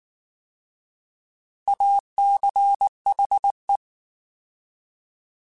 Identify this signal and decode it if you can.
Morse code, heard as a keyed tone, about 19 wpm, 800 Hz. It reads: ACHE